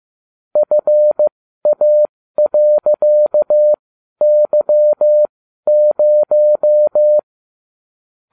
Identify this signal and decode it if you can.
Morse code, heard as a keyed tone, about 15 wpm, spaced as for 12 wpm, 600 Hz. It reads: FA.Y0